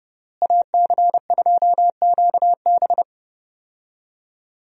Morse code, heard as a keyed tone, about 30 wpm, 700 Hz. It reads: AC2QB
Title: USA Call Signs - 5 Characters - Repeated 2x - 30wpm